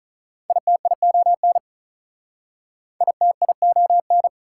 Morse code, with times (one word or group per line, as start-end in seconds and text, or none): 0.5-1.6 ITION
3.0-4.3 ITION